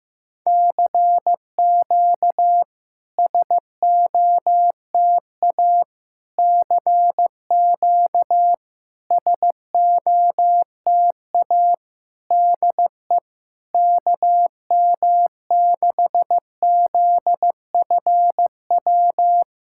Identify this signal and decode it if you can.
Morse code, heard as a keyed tone, about 15 wpm, 690 Hz. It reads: CQ SOTA CQ SOTA DE KM6ZFW
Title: SOTA QSO Practice 15wpm